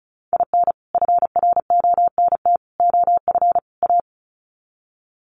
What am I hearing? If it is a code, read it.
Morse code, heard as a keyed tone, about 34 wpm, 700 Hz. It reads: IN FRONT OF A
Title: Sets of 4 Words 35wpm